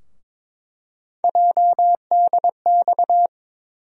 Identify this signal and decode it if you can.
Morse code, heard as a keyed tone, about 22 wpm, 700 Hz. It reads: JDX
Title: USA Call Sign Postfixes - 3 Letters 22wpm